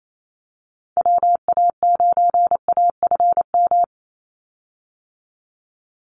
Morse code, heard as a keyed tone, about 28 wpm, 700 Hz. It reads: WA9AFM
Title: USA Call Signs 28wpm